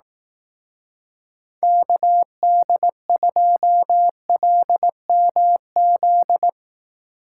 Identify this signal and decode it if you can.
Morse code, heard as a keyed tone, about 18 wpm, 700 Hz. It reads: KD2LMZ
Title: Speed Racing - USA Call Signs 15wpm